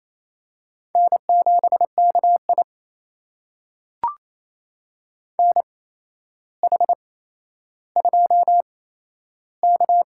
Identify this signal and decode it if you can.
Morse code, heard as a keyed tone, about 28 wpm, spaced as for 12 wpm, 700 Hz. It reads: N7KI NH2K
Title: USA Call Signs - 4 Characters - 8x Character Spacing - 28wpm